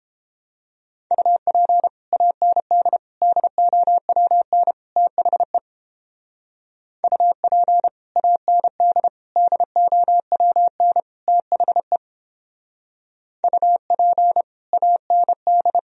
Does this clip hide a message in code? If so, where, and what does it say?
Yes: Morse code from 1.1 to 15.8 s, heard as a keyed tone, about 31 wpm, 700 Hz. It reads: UP AND DOWN THE UP AND DOWN THE UP AND